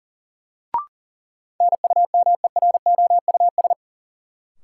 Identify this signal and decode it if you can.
Morse code, heard as a keyed tone, about 40 wpm, 700 Hz. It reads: NUMEROUS
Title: Sentences from Top 1400 Words - New Word Drill Alt - 40wpm